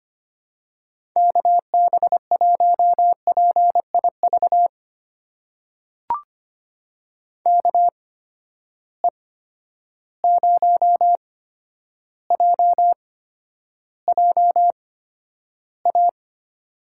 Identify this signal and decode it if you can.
Morse code, heard as a keyed tone, about 25 wpm, spaced as for 11 wpm, 700 Hz. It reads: KB1PIV KE0JJA
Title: USA Call Signs - 6 Characters - 8x Character Spacing - 25wpm